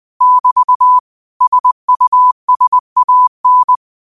Morse code, heard as a keyed tone, about 20 wpm, 1000 Hz. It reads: <BT> SUSAN
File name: qotdfiles / 20wpm